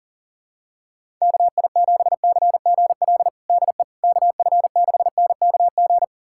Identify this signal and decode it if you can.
Morse code, heard as a keyed tone, about 40 wpm, 700 Hz. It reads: KI7CGL DE KF6NKG
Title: USA Callsign DE Callsign - 6 Characters - 40wpm